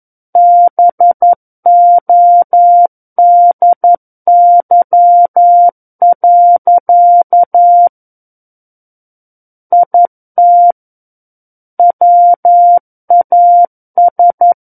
Morse code, heard as a keyed tone, about 11 wpm, 700 Hz. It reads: BODY. IT WAS
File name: 11 WPM